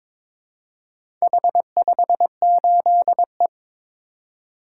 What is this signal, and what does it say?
Morse code, heard as a keyed tone, about 22 wpm, 700 Hz. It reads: H58E
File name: practice / 22wpm / Four Letter-Number Combinations - 22wpm